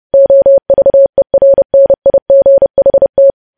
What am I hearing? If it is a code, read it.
Morse code, heard as a keyed tone, about 30 wpm, 550 Hz. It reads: OVERNIGHT
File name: podcast / News headlines-1x-30x30